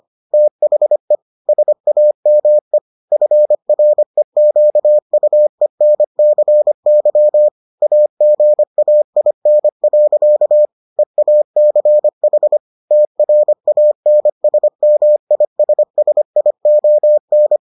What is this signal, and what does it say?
Morse code, heard as a keyed tone, about 25 wpm, 600 Hz. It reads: THE SAME FREQUENCY AGAIN. EACH TRANSMISSION